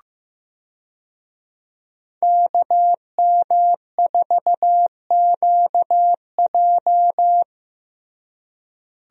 Morse code, heard as a keyed tone, about 15 wpm, 700 Hz. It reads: KM4QJ
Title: USA Call Signs - 5 Characters - Repeated 2x - 15wpm